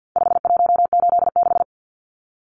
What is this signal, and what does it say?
Morse code, heard as a keyed tone, about 50 wpm, 700 Hz. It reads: 5186
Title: Numbers - 4-Digit 50wpm